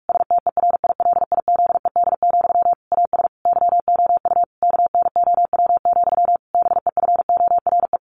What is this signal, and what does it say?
Morse code, heard as a keyed tone, about 44 wpm, 700 Hz. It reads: STERILIZED, AS YOU KNOW, BEFORE
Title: A Sound Of Thunder 45wpm